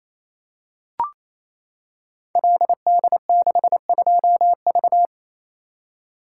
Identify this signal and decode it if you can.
Morse code, heard as a keyed tone, about 27 wpm, 700 Hz. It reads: LD62V
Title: Code Groups - 5 Letter:Numbers - 28wpm